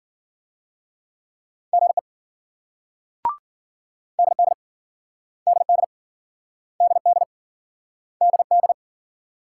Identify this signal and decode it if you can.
Morse code, heard as a keyed tone, about 53 wpm, 700 Hz. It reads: GE DD DD DD DD